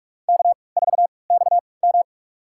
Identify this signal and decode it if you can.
Morse code, heard as a keyed tone, about 46 wpm, spaced as for 25 wpm, 700 Hz. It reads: K4XM
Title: USA Call Signs - 4 Characters - 3x Character Spacing - 45wpm